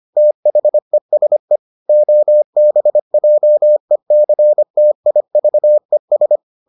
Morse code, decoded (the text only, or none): THESE OBJECTIVES